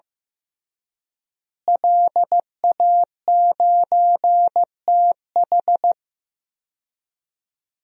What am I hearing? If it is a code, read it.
Morse code, heard as a keyed tone, about 15 wpm, 700 Hz. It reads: LA9TH
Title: DXCC-100 Call Signs - 5 Characters - Repeated 3x - 15wpm